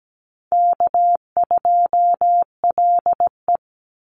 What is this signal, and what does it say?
Morse code, heard as a keyed tone, about 17 wpm, 700 Hz. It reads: K2LE
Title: Easy Call Signs Repeated 3x - 17wpm